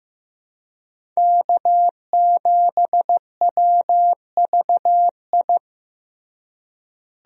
Morse code, heard as a keyed tone, about 15 wpm, 700 Hz. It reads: K7WVI
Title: USA Call Signs - 5 Characters - Repeated 2x - 15wpm